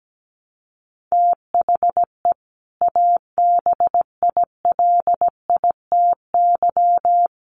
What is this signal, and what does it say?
Morse code, heard as a keyed tone, about 17 wpm, 700 Hz. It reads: THE ABILITY
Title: Sets of 3 Words 17wpm